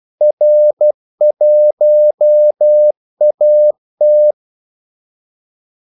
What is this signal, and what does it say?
Morse code, heard as a keyed tone, about 12 wpm, 600 Hz. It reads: R1AT
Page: 2 ekstra ordmellemrum mellem kaldesignalerne